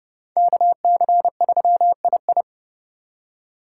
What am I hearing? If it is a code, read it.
Morse code, heard as a keyed tone, about 36 wpm, 700 Hz. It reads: KC3II
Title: Speed Racing - USA 2x2 Call Signs 30wpm